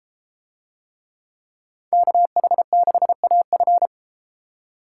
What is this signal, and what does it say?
Morse code, heard as a keyed tone, about 31 wpm, 700 Hz. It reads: KH6AF